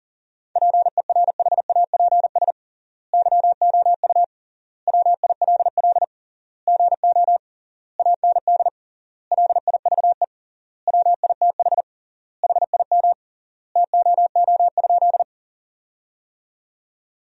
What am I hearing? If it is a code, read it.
Morse code, heard as a keyed tone, about 40 wpm, 700 Hz. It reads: PERHAPS YOU WILL GO AND LIVE WITH HIM TOO?
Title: 3x Word-Spacing - Sentences from Top 500 Words - 40wpm